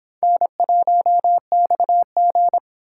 Morse code, heard as a keyed tone, about 25 wpm, 700 Hz. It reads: N1XG